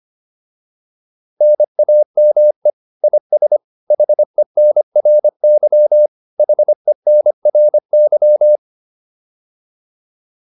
Morse code, heard as a keyed tone, about 25 wpm, 600 Hz. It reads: NAME IS HENRY HENRY